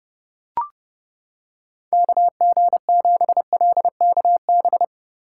Morse code, heard as a keyed tone, about 30 wpm, 700 Hz. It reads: KG7LKB